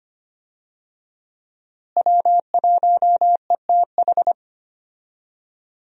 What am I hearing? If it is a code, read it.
Morse code, heard as a keyed tone, about 25 wpm, 700 Hz. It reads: W1ETH
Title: USA Call Signs - 5 Characters - Repeated 2x - 25wpm